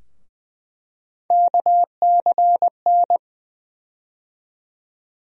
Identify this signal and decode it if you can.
Morse code, heard as a keyed tone, about 20 wpm, 700 Hz. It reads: KCN
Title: USA Call Sign Postfixes - 3 Letters 20wpm